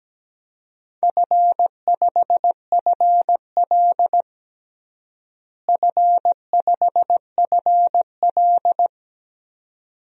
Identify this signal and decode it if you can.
Morse code, heard as a keyed tone, about 17 wpm, 700 Hz. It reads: F5FL F5FL
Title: DXCC-100 Call Signs - 4 Characters - Repeated 2x - 17wpm